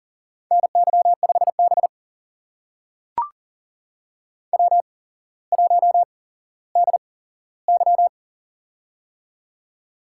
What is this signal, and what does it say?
Morse code, heard as a keyed tone, about 38 wpm, spaced as for 17 wpm, 700 Hz. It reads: NY5B W1DY
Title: USA Call Signs - 4 Characters - 8x Character Spacing - 40wpm